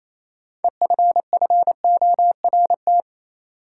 Morse code, heard as a keyed tone, about 28 wpm, 700 Hz. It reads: EFFORT
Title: Sentences from Top 1200 Words - New Word Drill - 28wpm